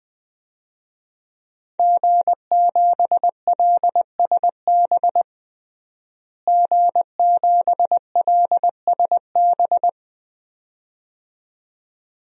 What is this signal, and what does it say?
Morse code, heard as a keyed tone, about 21 wpm, 700 Hz. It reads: G7LSB G7LSB
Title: DXCC-100 Call Signs - 5 Characters - Repeated 2x - 20wpm